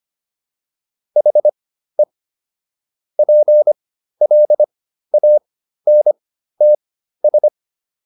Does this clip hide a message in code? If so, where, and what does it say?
Yes: Morse code from 1.2 to 7.5 s, heard as a keyed tone, about 25 wpm, spaced as for 13 wpm, 600 Hz. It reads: HE PLANTS